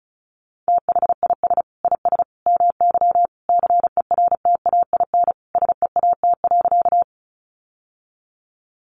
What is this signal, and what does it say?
Morse code, heard as a keyed tone, about 34 wpm, 700 Hz. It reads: THIS IS MY CERTAIN SEAT.
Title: Sentences from Top 1500 Words 35wpm